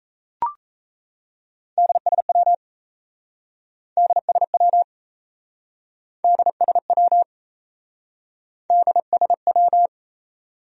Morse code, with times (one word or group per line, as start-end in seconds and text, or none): 1.8-2.5 DSW
4.0-4.8 DSW
6.2-7.2 DSW
8.7-9.9 DSW